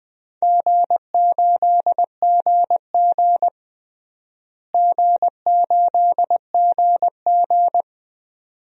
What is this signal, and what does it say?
Morse code, heard as a keyed tone, about 20 wpm, 700 Hz. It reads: G8GG G8GG
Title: DXCC-100 Call Signs - 4 Characters - Repeated 2x - 20wpm